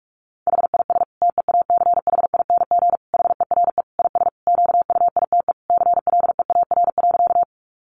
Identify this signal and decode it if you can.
Morse code, heard as a keyed tone, about 46 wpm, 700 Hz. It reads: HIS TEACHING HERE IS QUITE CLEAR.
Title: Sentences from Top 1100 Words 45wpm